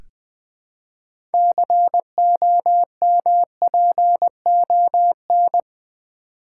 Morse code, heard as a keyed tone, about 20 wpm, 700 Hz. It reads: COMPON
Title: Build A Word - 3 Letters At A Time - 9-Letter Words - 20wpm